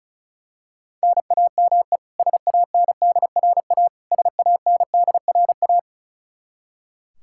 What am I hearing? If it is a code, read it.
Morse code, heard as a keyed tone, about 36 wpm, 700 Hz. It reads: NAME SANDRA SANDRA
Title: Practice QSO - Exchange 1 - Beginning - 35wpm